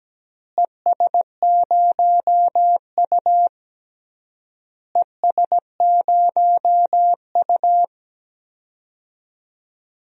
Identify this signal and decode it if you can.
Morse code, heard as a keyed tone, about 17 wpm, 700 Hz. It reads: ES0U ES0U
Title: DXCC-100 Call Signs - 4 Characters - Repeated 3x - 17wpm